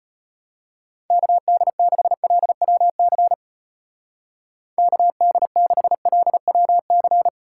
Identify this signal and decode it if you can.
Morse code, heard as a keyed tone, about 31 wpm, 700 Hz. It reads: KD6LWC KD6LWC
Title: Speed Racing - USA 2x3 Call Signs 25wpm